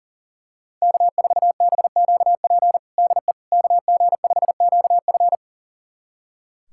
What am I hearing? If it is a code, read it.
Morse code, heard as a keyed tone, about 37 wpm, 700 Hz. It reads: K4BQP DE KG5QF
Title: USA Callsign DE Callsign 40wpm